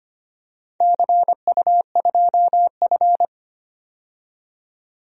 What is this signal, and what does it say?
Morse code, heard as a keyed tone, about 25 wpm, 700 Hz. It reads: CU2F
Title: DXCC-100 Call Signs - 4 Characters - 25wpm